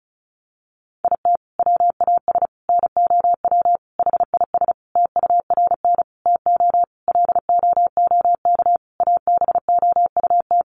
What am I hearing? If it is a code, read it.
Morse code, heard as a keyed tone, about 34 wpm, 700 Hz. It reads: IT WAS NOW HIS TURN TO LOOK ABOUT
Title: Sentences from Top 200 Words 35wpm